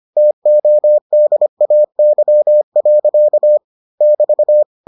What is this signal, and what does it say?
Morse code, heard as a keyed tone, about 25 wpm, 600 Hz. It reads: TODAY. <BT>